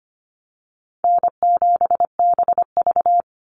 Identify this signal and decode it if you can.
Morse code, heard as a keyed tone, about 25 wpm, 700 Hz. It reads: N7BV
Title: Easy Call Signs - 25wpm